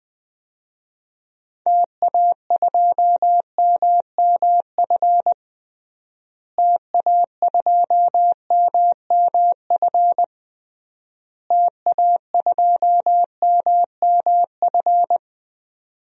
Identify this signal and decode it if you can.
Morse code, heard as a keyed tone, about 21 wpm, 700 Hz. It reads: TA2MMF TA2MMF TA2MMF
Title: DXCC-100 Call Signs - 6 Characters - Repeated 3x - 20wpm